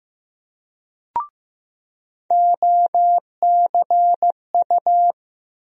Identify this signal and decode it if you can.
Morse code, heard as a keyed tone, about 15 wpm, 700 Hz. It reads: OCU
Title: USA Call Sign Postfixes - 3 Letters 15wpm